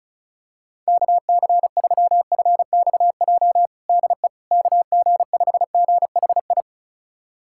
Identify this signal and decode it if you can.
Morse code, heard as a keyed tone, about 34 wpm, 700 Hz. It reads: KC3FXJ DE KG5GHI
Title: USA Callsign DE Callsign 35wpm